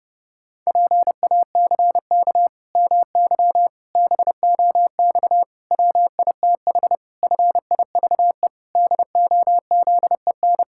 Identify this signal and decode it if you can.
Morse code, heard as a keyed tone, about 30 wpm, 700 Hz. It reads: PACK MY BOX WITH FIVE DOZEN
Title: Pangram Challenge v2 - 30wpm